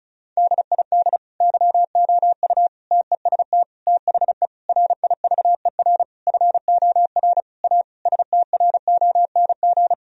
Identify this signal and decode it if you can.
Morse code, heard as a keyed tone, about 36 wpm, 700 Hz. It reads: DID YOU TEST THE RIVER FOR A STRONG
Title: Sentences from Top 1000 Words - Review Part 4 - 35wpm